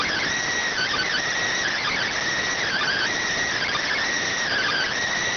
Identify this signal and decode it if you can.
Emergency Alert System (SAME) tones (header bursts), heard as harsh buzzing bursts. It is heard from 0.7 to 4.9 s.